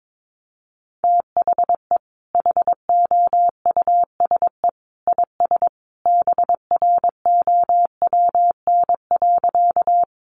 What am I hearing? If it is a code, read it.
Morse code, heard as a keyed tone, about 22 wpm, 700 Hz. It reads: THE HOUSE IS BROWN.
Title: Sentences from Top 1200 Words 22wpm